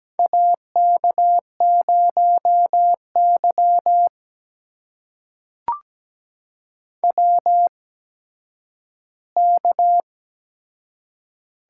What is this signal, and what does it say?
Morse code, heard as a keyed tone, about 17 wpm, spaced as for 10 wpm, 700 Hz. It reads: AK0Y WK
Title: USA Call Signs - 4 Characters - 8x Character Spacing - 17wpm